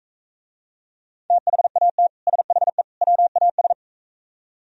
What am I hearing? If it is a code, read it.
Morse code, heard as a keyed tone, about 45 wpm, 700 Hz. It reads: THAT SHE WAS